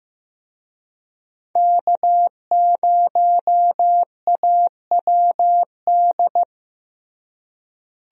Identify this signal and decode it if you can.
Morse code, heard as a keyed tone, about 15 wpm, 700 Hz. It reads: K0AWD
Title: USA Call Signs - 5 Characters - Repeated 2x - 15wpm